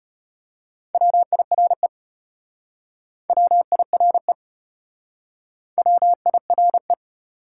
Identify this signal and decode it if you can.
Morse code, heard as a keyed tone, about 34 wpm, 700 Hz. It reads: WIRE WIRE WIRE